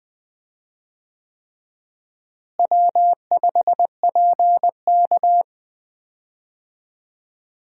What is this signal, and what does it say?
Morse code, heard as a keyed tone, about 21 wpm, 700 Hz. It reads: W5PK
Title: USA Call Signs - 4 Characters - Repeated 3x - 20wpm